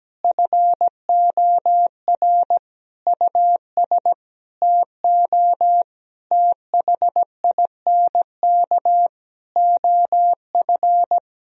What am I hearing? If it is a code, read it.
Morse code, heard as a keyed tone, about 17 wpm, 700 Hz. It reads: FOR US TO THINK OF
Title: Sentences from Top 1000 Words - Review Part 3 - 17wpm